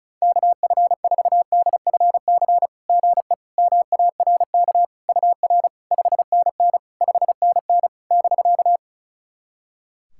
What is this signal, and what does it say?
Morse code, heard as a keyed tone, about 36 wpm, 690 Hz. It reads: KF4DFC GE MARK UR 5NN 5NN <BK>
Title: SOTA QSO Practice 35wpm